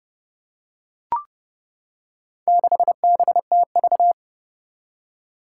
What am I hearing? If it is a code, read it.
Morse code, heard as a keyed tone, about 30 wpm, 700 Hz. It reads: 6BTV
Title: Four Challenging Letter-Number Combinations 30wpm